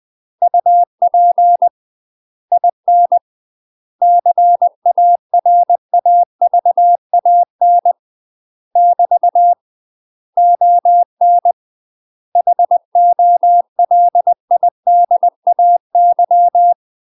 Morse code, heard as a keyed tone, about 20 wpm, 700 Hz. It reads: UP IN CARAVAN <BT> ON HOLIDAY
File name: UK MASTER QSOs TRAINING / 20WPM / UK MASTER QSOs 0062